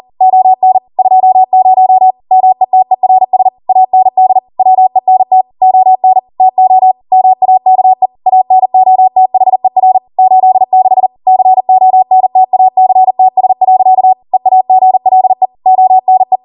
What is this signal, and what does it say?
Morse code, heard as a keyed tone, about 40 wpm, 750 Hz. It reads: ON 20 METERS AND WENT ON TO MAKE ANOTHER 86 CONTACTS. EAGLE ONE